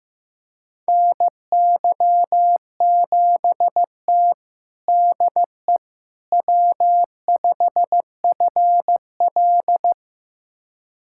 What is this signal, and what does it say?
Morse code, heard as a keyed tone, about 15 wpm, 700 Hz. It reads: NY7T DE W5FL
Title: USA Callsign DE Callsign - 4 Characters - 15wpm